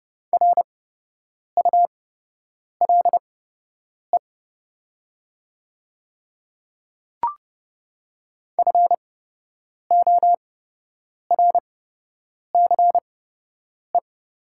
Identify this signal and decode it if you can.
Morse code, heard as a keyed tone, about 28 wpm, spaced as for 8 wpm, 700 Hz. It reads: RULE FORCE
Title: 8x Character Spacing - Top 500 Words - 30wpm